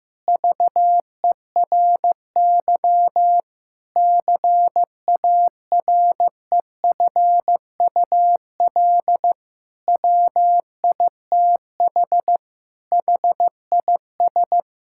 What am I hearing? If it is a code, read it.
Morse code, heard as a keyed tone, about 15 wpm, 700 Hz. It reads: VERY CAREFUL WITH HIS
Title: Sentences from Top 1600 Words 15wpm